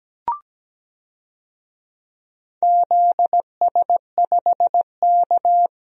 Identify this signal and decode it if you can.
Morse code, heard as a keyed tone, about 17 wpm, 700 Hz. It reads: ZS5K
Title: DXCC-100 Call Signs - 4 Characters - Repeated 3x - 17wpm